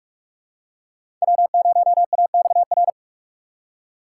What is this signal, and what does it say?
Morse code, heard as a keyed tone, about 42 wpm, 700 Hz. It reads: W0AXR
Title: USA Call Signs - 3x Character Spacing - 45wpm